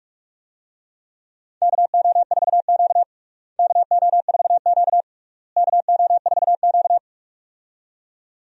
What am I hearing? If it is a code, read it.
Morse code, heard as a keyed tone, about 46 wpm, 700 Hz. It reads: KO4Q KO4Q KO4Q
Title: USA Call Signs - 4 Characters - Repeated 3x - 45wpm